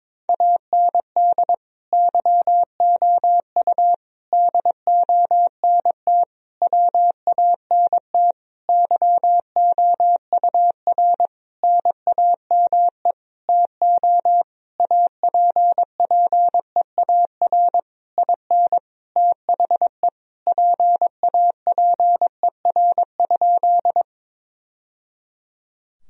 Morse code, heard as a keyed tone, about 22 wpm, 700 Hz. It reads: AND YOU DONT WANT YOUR NAME TO APPEAR IN THE PAPER?
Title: Sentences from Top 1000 Words - Review Part 2 - 22wpm